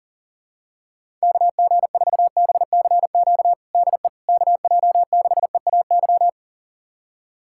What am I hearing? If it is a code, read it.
Morse code, heard as a keyed tone, about 37 wpm, 700 Hz. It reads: KG4BCQ DE KJ6EAY
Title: USA Callsign DE Callsign - 6 Characters - 40wpm